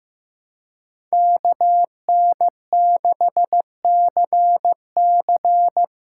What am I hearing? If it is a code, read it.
Morse code, heard as a keyed tone, about 15 wpm, 700 Hz. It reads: KN6CC
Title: USA Call Signs - 5 Characters - 2x Character Spacing - 15wpm